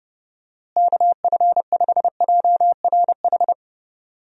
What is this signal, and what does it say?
Morse code, heard as a keyed tone, about 30 wpm, 700 Hz. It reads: KF5JRH
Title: USA Call Signs - 6 Characters - Repeated 2x - 30wpm